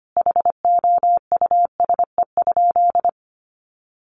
Morse code, heard as a keyed tone, about 25 wpm, 700 Hz. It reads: HOUSE?
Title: Sentences from Top 1400 Words 25wpm